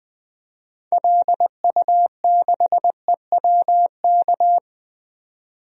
Morse code, heard as a keyed tone, about 20 wpm, 700 Hz. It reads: LU6EWK
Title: DXCC-100 Call Signs - 6 Characters - 20wpm